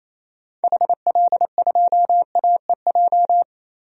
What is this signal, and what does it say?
Morse code, heard as a keyed tone, about 28 wpm, 700 Hz. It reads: HL2AEJ
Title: DXCC-100 Call Signs - 6 Characters - 28wpm